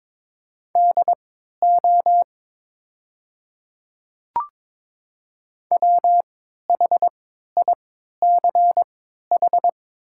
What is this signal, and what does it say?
Morse code, heard as a keyed tone, about 22 wpm, spaced as for 12 wpm, 700 Hz. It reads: DO WHICH
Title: 3x Character Spacing - Top 200 Words - 22wpm